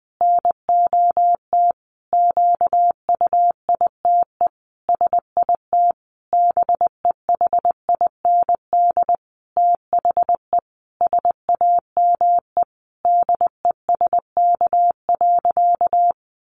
Morse code, heard as a keyed tone, about 20 wpm, 700 Hz. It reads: NOT QUITE SIT BEHIND THE SAME DESK.
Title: A Sound Of Thunder 20wpm